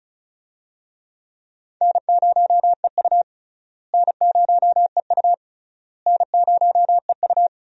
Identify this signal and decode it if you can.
Morse code, heard as a keyed tone, about 36 wpm, 700 Hz. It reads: N0EU N0EU N0EU
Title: USA Call Signs - 4 Characters - Repeated 3x - 35wpm